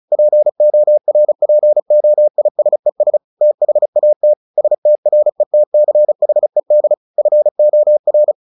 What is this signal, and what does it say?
Morse code, heard as a keyed tone, about 34 wpm, 600 Hz. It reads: PORPOISES THAT STRETCHED FOR
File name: the old man and the sea 2 35wpm